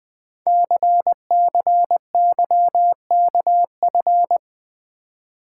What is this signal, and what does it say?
Morse code, heard as a keyed tone, about 20 wpm, 700 Hz. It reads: CCYKF
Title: Code Groups - 5 Letters - 20wpm